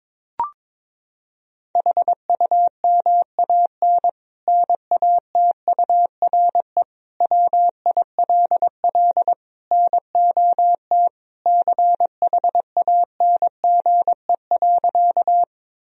Morse code, heard as a keyed tone, about 22 wpm, 700 Hz. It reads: HUMAN NATURE WILL NOT CHANGE.
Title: Sentences from Top 1000 Words - Review Part 3 - 22wpm